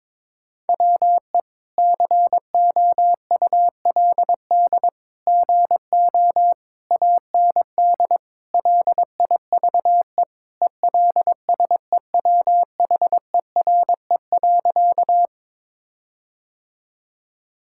Morse code, heard as a keyed tone, about 23 wpm, 700 Hz. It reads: WE COULD GO AND LIVE ELSEWHERE.
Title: Sentences from Top 1600 Words 22wpm